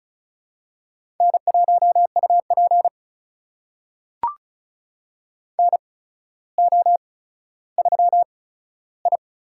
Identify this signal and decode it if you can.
Morse code, heard as a keyed tone, about 36 wpm, spaced as for 15 wpm, 700 Hz. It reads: N1UP NO3I
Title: USA Call Signs - 4 Characters - 8x Character Spacing - 35wpm